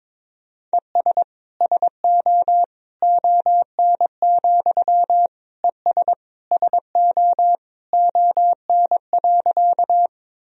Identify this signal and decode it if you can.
Morse code, heard as a keyed tone, about 22 wpm, 700 Hz. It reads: ES SO ON, ES SO ON.
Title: Sentences from Top 100 Words and Abbreviations 22wpm